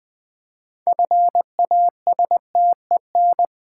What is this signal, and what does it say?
Morse code, heard as a keyed tone, about 20 wpm, 700 Hz. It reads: FASTEN
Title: 6 Letter Words - First 3 Letters Given - 20wpm